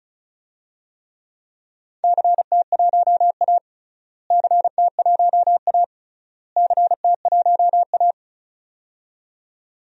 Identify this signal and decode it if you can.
Morse code, heard as a keyed tone, about 36 wpm, 700 Hz. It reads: CT1A CT1A CT1A
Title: DXCC-100 Call Signs - 4 Characters - Repeated 3x - 35wpm